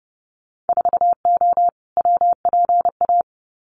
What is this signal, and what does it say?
Morse code, heard as a keyed tone, about 30 wpm, 700 Hz. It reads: 4O WPA
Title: Winter Field Day Exchanges 30wpm